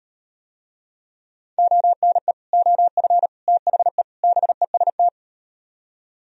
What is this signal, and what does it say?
Morse code, heard as a keyed tone, about 38 wpm, 700 Hz. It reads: ONE OF THE BEST